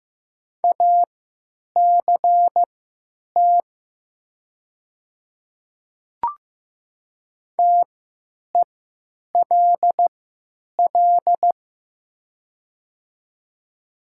Morse code, heard as a keyed tone, about 15 wpm, spaced as for 9 wpm, 700 Hz. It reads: ACT TELL
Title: 3x Character Spacing - Top 200 Words - 15wpm